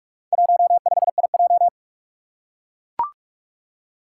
Morse code, heard as a keyed tone, about 44 wpm, 700 Hz. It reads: 15IJ
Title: Four Challenging Letter-Number Combinations 45wpm